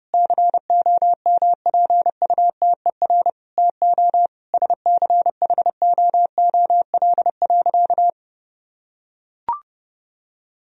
Morse code, heard as a keyed tone, about 30 wpm, 700 Hz. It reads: COMPUTER TO SCHOOL.
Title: Sentences from Top 1000 Words - Review Part 3 - 30wpm